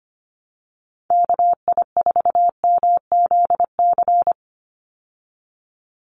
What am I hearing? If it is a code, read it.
Morse code, heard as a keyed tone, about 25 wpm, 700 Hz. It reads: KI4MZC
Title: Call Signs Repeated 2x - 25wpm